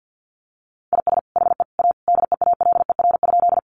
Morse code, heard as a keyed tone, about 50 wpm, 700 Hz. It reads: IS HE A DEALER?
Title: Sentences from Top 1100 Words 50wpm